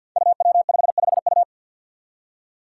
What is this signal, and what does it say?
Morse code, heard as a keyed tone, about 52 wpm, 700 Hz. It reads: UW55U
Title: Five Challenging Letter-Number Combinations 50wpm